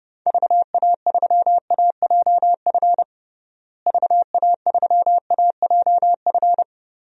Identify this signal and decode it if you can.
Morse code, heard as a keyed tone, about 32 wpm, 700 Hz. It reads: VA3AJF VA3AJF
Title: DXCC-100 Call Signs - 6 Characters - Repeated 3x - 30wpm